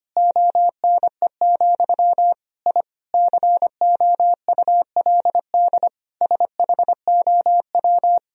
Morse code, heard as a keyed tone, about 25 wpm, 700 Hz. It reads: ONE, I COULD SHOW
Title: Sentences from Top 1000 Words - Review Part 3 - 25wpm